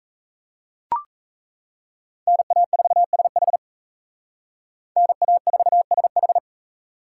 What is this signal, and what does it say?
Morse code, heard as a keyed tone, about 33 wpm, 700 Hz. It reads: NA4SH NA4SH
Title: Speed Racing - USA 2x2 Call Signs 28wpm